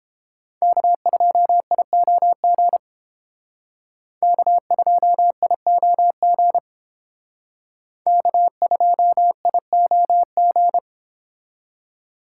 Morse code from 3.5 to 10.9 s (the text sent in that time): K2IOG K2IOG